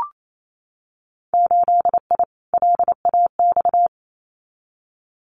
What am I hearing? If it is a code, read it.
Morse code, heard as a keyed tone, about 28 wpm, 690 Hz. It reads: 8I LAX
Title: Winter Field Day Exchanges Encore 28wpm